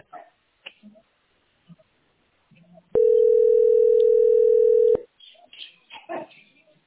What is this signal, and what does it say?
Ringback tone (North America), 440 + 480 Hz.
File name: 43da-48ab-b100